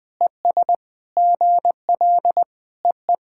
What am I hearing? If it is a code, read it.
Morse code, heard as a keyed tone, about 21 wpm, 700 Hz. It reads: ES GL EE
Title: SOTA QSO Practice 20wpm